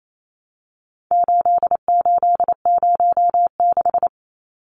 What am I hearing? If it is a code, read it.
Morse code, heard as a keyed tone, about 28 wpm, 700 Hz. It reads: 8806
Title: Numbers - 4-Digit 28wpm